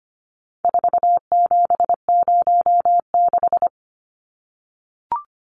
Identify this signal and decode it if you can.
Morse code, heard as a keyed tone, about 25 wpm, 700 Hz. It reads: 4706
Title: Numbers - 4-Digit 25wpm